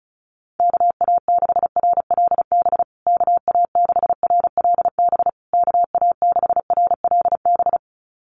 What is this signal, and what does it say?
Morse code, heard as a keyed tone, about 34 wpm, 700 Hz. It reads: KA6RLB KA6RLB KA6RLB
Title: USA Call Signs Repeated 3x 35wpm